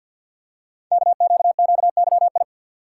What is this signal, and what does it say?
Morse code, heard as a keyed tone, about 51 wpm, 700 Hz. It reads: KQQYI
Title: Code Groups - 5 Letter:Numbers - 50wpm